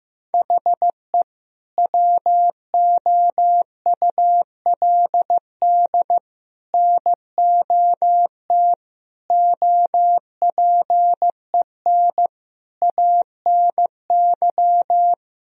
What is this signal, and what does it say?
Morse code, heard as a keyed tone, about 15 wpm, 700 Hz. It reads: HE WOULD NOT OPEN ANY
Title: Sentences from Top 1000 Words - Review Part 1 - 15wpm